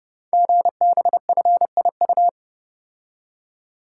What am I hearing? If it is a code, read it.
Morse code, heard as a keyed tone, about 30 wpm, 700 Hz. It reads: GBFIU
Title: Code Groups - 5 Letters - 30wpm